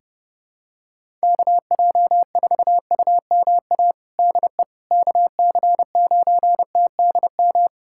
Morse code, heard as a keyed tone, about 30 wpm, 700 Hz. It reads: KJ4UMA DE KC9TDM
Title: USA Callsign DE Callsign - 6 Characters - 30wpm